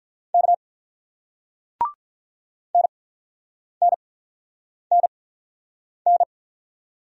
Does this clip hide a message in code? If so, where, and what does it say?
Yes: Morse code from 0.3 to 6.2 s, heard as a keyed tone, about 46 wpm, spaced as for 9 wpm, 700 Hz. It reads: K NNNN